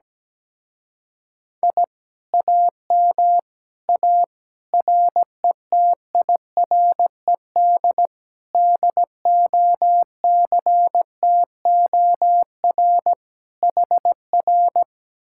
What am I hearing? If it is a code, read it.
Morse code, heard as a keyed tone, about 17 wpm, 700 Hz. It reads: I AM A RETIRED DOCTOR HR
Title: Speed-Racing Common QSO Phrases - 17wpm